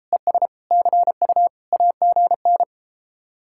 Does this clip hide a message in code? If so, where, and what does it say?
Yes: Morse code from 0.1 to 2.6 s, heard as a keyed tone, about 33 wpm, 700 Hz. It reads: ES CU AGN